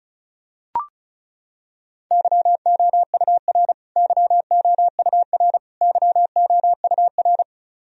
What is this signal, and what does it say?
Morse code, heard as a keyed tone, about 34 wpm, 700 Hz. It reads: YOUR YOUR YOUR
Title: Top 100 Words Repeated 35wpm